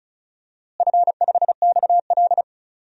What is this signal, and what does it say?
Morse code, heard as a keyed tone, about 34 wpm, 700 Hz. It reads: F5XL
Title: DXCC-100 Call Signs - 4 Characters - Repeated 3x - 35wpm